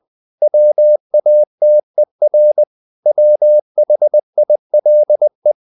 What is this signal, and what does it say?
Morse code, heard as a keyed tone, about 20 wpm, 600 Hz. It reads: WATER WHILE
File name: the old man and the sea part 2 20wpm